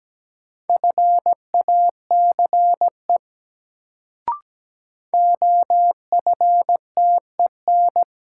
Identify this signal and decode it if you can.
Morse code, heard as a keyed tone, about 17 wpm, 700 Hz. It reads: FACE OFTEN
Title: Sentences from Top 300 Words - New Word Drill - 17wpm